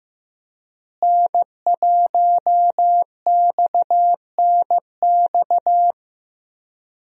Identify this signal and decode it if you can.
Morse code, heard as a keyed tone, about 15 wpm, 700 Hz. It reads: N1XNX
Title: USA Call Signs - 5 Characters - Repeated 3x - 15wpm